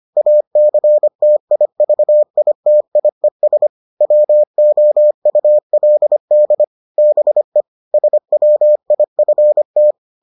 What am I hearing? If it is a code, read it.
Morse code, heard as a keyed tone, about 25 wpm, 600 Hz. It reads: ACTIVITIES WOULD BE SWIFT